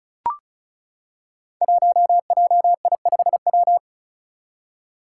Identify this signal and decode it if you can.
Morse code, heard as a keyed tone, about 36 wpm, 700 Hz. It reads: 1JI5W
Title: Five Challenging Letter-Number Combinations 35wpm